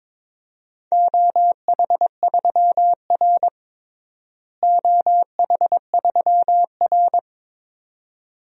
Morse code, heard as a keyed tone, about 22 wpm, 700 Hz. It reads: OH3R OH3R
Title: DXCC-100 Call Signs - 4 Characters - Repeated 2x - 22wpm